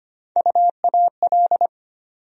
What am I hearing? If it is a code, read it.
Morse code, heard as a keyed tone, about 25 wpm, 700 Hz. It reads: UAL
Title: USA Call Sign Postfixes - 3 Letters 25wpm